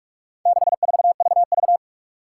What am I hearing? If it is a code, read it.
Morse code, heard as a keyed tone, about 44 wpm, 700 Hz. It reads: 64VV
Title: Four Challenging Letter-Number Combinations 45wpm